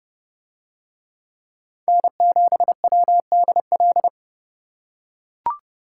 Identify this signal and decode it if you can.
Morse code, heard as a keyed tone, about 30 wpm, 700 Hz. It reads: N7WDL